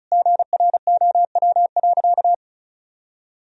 Morse code, heard as a keyed tone, about 36 wpm, 700 Hz. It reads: GROW.
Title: Sentences from Top 1000 Words - Review Part 1 - 35wpm